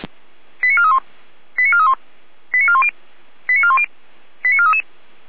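Five-tone selective call (ZVEI), heard as quick runs of five tones, tones of about 80 ms. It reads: 89431 89431 89410 89410 89433
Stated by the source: ZVEI five-tone selective call 89160 received by SDR, logged by Feuerwehr Orlamünde.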